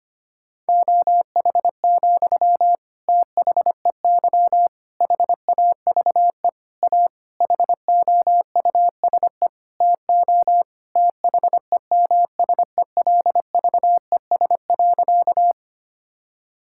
Morse code, heard as a keyed tone, about 25 wpm, 700 Hz. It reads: OH, THEY HAVE A HOUSE TO THEMSELVES.
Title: Sentences from Top 1000 Words - Review Part 5 - 25wpm